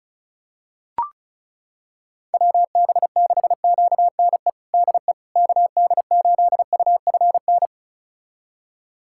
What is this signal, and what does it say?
Morse code, heard as a keyed tone, about 36 wpm, 690 Hz. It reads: WB6QNE DE KD8UFN
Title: USA Callsign DE Callsign - 6 Characters - 35wpm